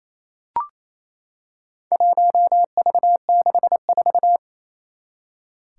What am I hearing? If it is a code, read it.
Morse code, heard as a keyed tone, about 28 wpm, 700 Hz. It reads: 1V64
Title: Four Challenging Letter-Number Combinations 28wpm